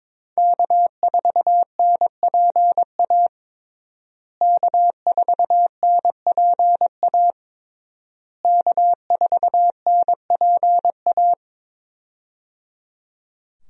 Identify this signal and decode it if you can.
Morse code, heard as a keyed tone, about 22 wpm, 700 Hz. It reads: K4NPA K4NPA K4NPA
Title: USA Call Signs - 5 Characters - Repeated 3x - 22wpm